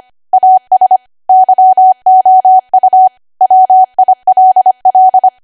Morse code, heard as a keyed tone, about 25 wpm, 750 Hz. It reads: AS YOU WILL
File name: cwa / W1AW-Level3 / 25WPM